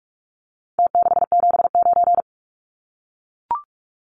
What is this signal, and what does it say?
Morse code, heard as a keyed tone, about 44 wpm, 700 Hz. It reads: T679
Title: Numbers - Contest Style - 4-Digit 45wpm